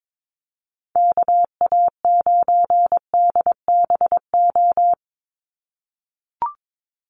Morse code, heard as a keyed tone, about 22 wpm, 700 Hz. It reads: KA9DBO
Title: USA Call Signs 22wpm